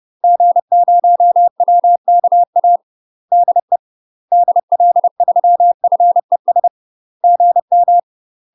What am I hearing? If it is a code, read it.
Morse code, heard as a keyed tone, about 30 wpm, 700 Hz. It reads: G0WKA DE DL3FES GM
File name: UK MASTER QSOs TRAINING / 30WPM / UK MASTER QSOs 0018